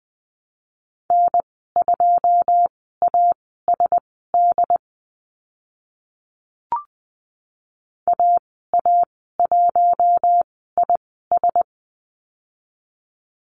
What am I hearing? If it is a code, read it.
Morse code, heard as a keyed tone, about 20 wpm, spaced as for 14 wpm, 700 Hz. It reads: N2ASD AA1IS
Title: 082 Lesson 14g - Number 2 - Call Signs 20wpm